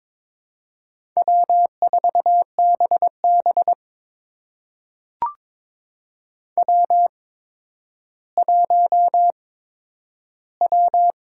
Morse code, heard as a keyed tone, about 22 wpm, spaced as for 11 wpm, 700 Hz. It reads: W4BB W1W
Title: USA Call Signs - 8x Character Spacing - 22wpm